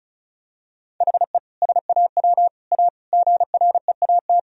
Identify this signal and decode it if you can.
Morse code, heard as a keyed tone, about 36 wpm, 700 Hz. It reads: HE SAW A GREAT
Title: Sentences from Top 1000 Words - Review Part 1 - 35wpm